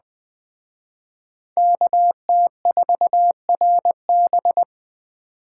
Morse code, heard as a keyed tone, about 20 wpm, 700 Hz. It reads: KT4RB